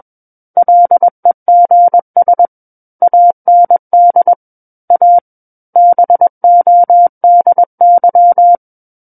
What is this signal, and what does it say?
Morse code, heard as a keyed tone, about 21 wpm, 700 Hz. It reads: LEGS AND A BODY
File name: Book One / 12-Chapter 12 - 21 WPM